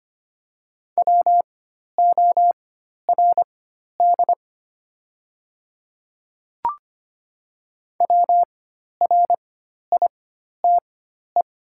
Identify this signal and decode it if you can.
Morse code, heard as a keyed tone, about 25 wpm, spaced as for 12 wpm, 700 Hz. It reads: WORD WRITE
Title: 4x Character Spacing - Top 100 Words - 25wpm